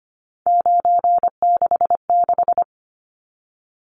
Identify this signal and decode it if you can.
Morse code, heard as a keyed tone, about 25 wpm, 700 Hz. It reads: 966
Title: Numbers - 3-Digit 25wpm